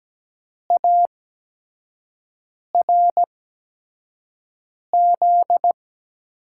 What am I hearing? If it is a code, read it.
Morse code, heard as a keyed tone, about 17 wpm, spaced as for 5 wpm, 700 Hz. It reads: ARZ
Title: USA Call Signs - 6 Characters - 8x Character Spacing - 17wpm